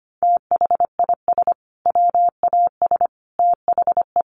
Morse code, heard as a keyed tone, about 25 wpm, 700 Hz. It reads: THIS WAS THE
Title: Sentences from Top 900 Words 25wpm